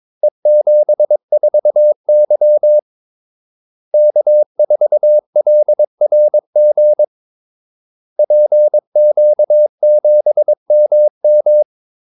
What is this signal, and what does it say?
Morse code, heard as a keyed tone, about 22 wpm, 600 Hz. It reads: E74Y K4LRG PQ7MM
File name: Callsigns-22wpm